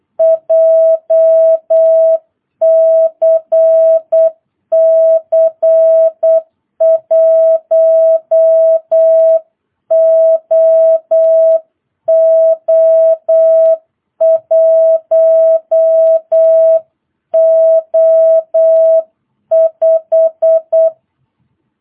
Morse code, heard as a keyed tone, about 8 wpm, 660 Hz. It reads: JCC 1OO1O5